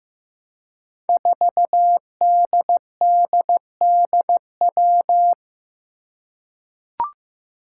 Morse code, heard as a keyed tone, about 15 wpm, 700 Hz. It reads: 4DDDW
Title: Five Challenging Letter-Number Combinations 15wpm